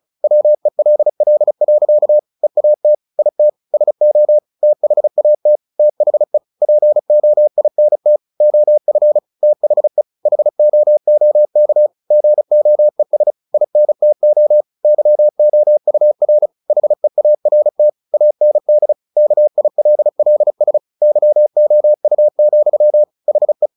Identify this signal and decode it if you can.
Morse code, heard as a keyed tone, about 34 wpm, 600 Hz. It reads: WELL. EAT IT SO THAT THE POINT OF THE HOOK GOES INTO YOUR HEART AND KILLS YOU, HE
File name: the old man and the sea / the old man and the sea 1 35wpm